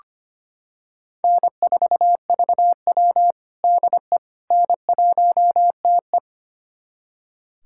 Morse code, heard as a keyed tone, about 25 wpm, 700 Hz. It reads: N4VW DE N1TE
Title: USA Callsign DE Callsign - 4 Characters - 25wpm